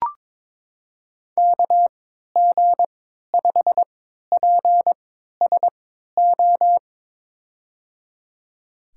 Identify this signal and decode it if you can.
Morse code, heard as a keyed tone, about 23 wpm, spaced as for 13 wpm, 700 Hz. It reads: KG5PSO
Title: USA Call Signs - 3x Character Spacing - 22wpm